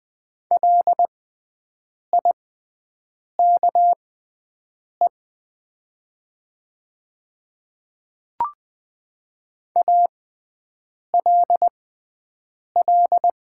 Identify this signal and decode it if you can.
Morse code, heard as a keyed tone, about 20 wpm, spaced as for 7 wpm, 700 Hz. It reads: LIKE ALL
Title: 6x Character Spacing - Top 100 Words - 20wpm